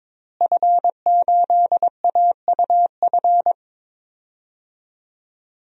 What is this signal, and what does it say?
Morse code, heard as a keyed tone, about 22 wpm, 700 Hz. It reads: F8AUF
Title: DXCC-100 Call Signs - 5 Characters - Repeated 2x - 22wpm